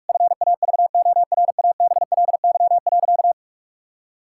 Morse code, heard as a keyed tone, about 46 wpm, 700 Hz. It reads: FAVORABLY.
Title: Pangram Challenge v2 - 45wpm